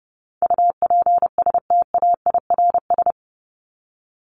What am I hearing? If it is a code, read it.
Morse code, heard as a keyed tone, about 30 wpm, 700 Hz. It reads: UPSTAIRS
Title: Top 1751-2000 Words 30wpm